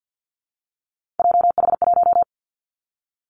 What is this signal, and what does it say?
Morse code, heard as a keyed tone, about 50 wpm, 700 Hz. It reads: JH1
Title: Callsign Prefixes for Top 25 DXCC - 50wpm